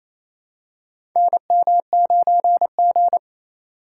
Morse code, heard as a keyed tone, about 28 wpm, 700 Hz. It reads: NM9G
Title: USA Call Signs - 4 Characters - 2x Character Spacing - 28wpm